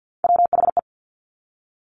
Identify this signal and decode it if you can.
Morse code, heard as a keyed tone, about 50 wpm, 700 Hz. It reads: WHE
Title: Top Three Letter Combinations in Words 50wpm